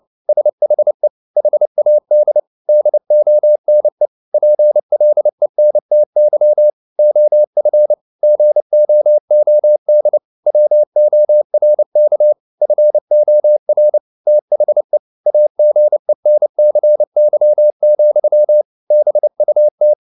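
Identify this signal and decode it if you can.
Morse code, heard as a keyed tone, about 29 wpm, 600 Hz. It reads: SHE HAD DONE PLENTY OF GOOD WORK FOR THE AGENCY, BUT
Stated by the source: Morse code at 29 wpm, sending Ken Follett's novel Never in English, from CW med Gnister.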